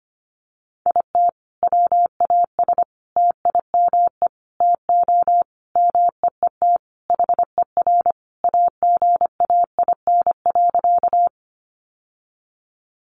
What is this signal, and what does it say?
Morse code, heard as a keyed tone, about 25 wpm, 700 Hz. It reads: IT WAS TIME TO MEET HER AGAIN.
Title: Sentences from Top 1100 Words 25wpm